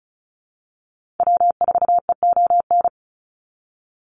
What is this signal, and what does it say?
Morse code, heard as a keyed tone, about 34 wpm, 700 Hz. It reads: W4EON
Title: NA SOTA Activator Callsigns 35wpm